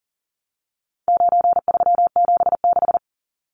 Morse code, heard as a keyed tone, about 40 wpm, 700 Hz. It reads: 9376
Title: Numbers - 4-Digit 40wpm